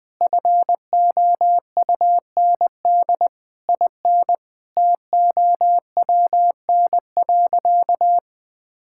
Morse code, heard as a keyed tone, about 20 wpm, 700 Hz. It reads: FOUND IN TOWN.
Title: Sentences from Top 1000 Words - Review Part 6 - 20wpm